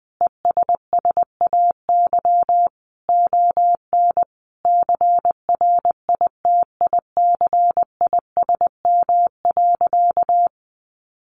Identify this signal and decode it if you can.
Morse code, heard as a keyed tone, about 20 wpm, 700 Hz. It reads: ESSAY ON CRITICISM.
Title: Sentences from Top 1500 Words 20wpm